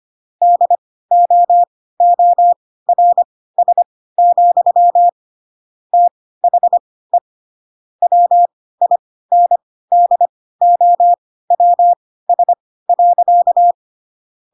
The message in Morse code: DOORS, THE WINDOWS.

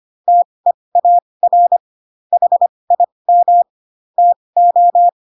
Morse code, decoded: TEAR HIM TO

